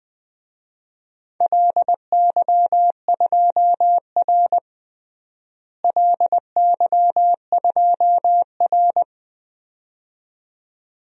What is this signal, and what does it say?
Morse code, heard as a keyed tone, about 20 wpm, 700 Hz. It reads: LY2R LY2R
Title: DXCC-100 Call Signs - 4 Characters - Repeated 2x - 20wpm